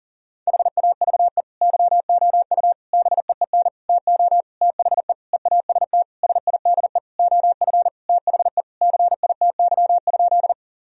Morse code, heard as a keyed tone, about 40 wpm, 700 Hz. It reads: HAVE YOU BEEN TO THE EAST SIDE OF THE CITY?
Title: Sentences from Top 1000 Words - Review Part 3 - 40wpm